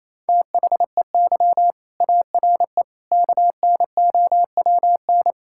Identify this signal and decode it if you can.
Morse code, heard as a keyed tone, about 28 wpm, 700 Hz. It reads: THEY ARE KNOWN
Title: Sentences from Top 1000 Words - Review Part 5 - 28wpm